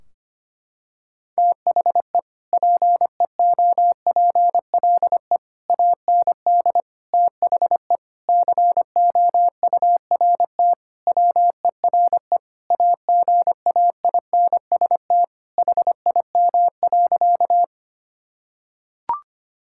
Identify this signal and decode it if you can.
Morse code, heard as a keyed tone, about 25 wpm, 700 Hz. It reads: THE PEOPLE AND THE COURT WERE AGAINST HIM.
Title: Sentences from Top 1600 Words 25wpm